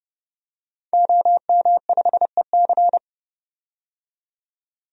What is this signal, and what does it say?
Morse code, heard as a keyed tone, about 30 wpm, 700 Hz. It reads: OM5EC
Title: DXCC-100 Call Signs - 5 Characters - 30wpm